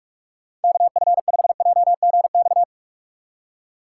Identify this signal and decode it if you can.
Morse code, heard as a keyed tone, about 45 wpm, 700 Hz. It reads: KF5JGX
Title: USA Call Signs - 6 Characters - 6x Character Spacing - 45wpm